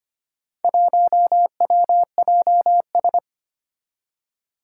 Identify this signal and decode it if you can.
Morse code, heard as a keyed tone, about 25 wpm, 700 Hz. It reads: 1WJS
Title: Four Challenging Letter-Number Combinations 25wpm